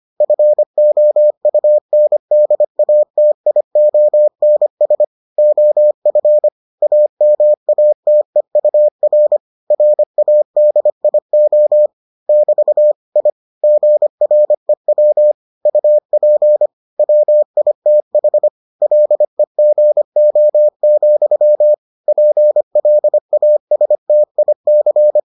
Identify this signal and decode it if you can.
Morse code, heard as a keyed tone, about 25 wpm, 600 Hz. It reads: FOUNDATIONS OF AMATEUR RADIO <BT> I GREW UP WITH LEGO, PLASTIC